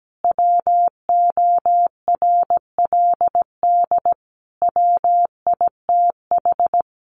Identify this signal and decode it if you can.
Morse code, heard as a keyed tone, about 17 wpm, 700 Hz. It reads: WORLD WITH
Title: Sentences from Top 1300 Words 17wpm